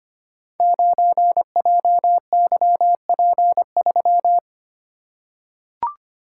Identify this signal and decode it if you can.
Morse code, heard as a keyed tone, about 25 wpm, 700 Hz. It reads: 9JYP3
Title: Code Groups - 5 Letter:Numbers - 25wpm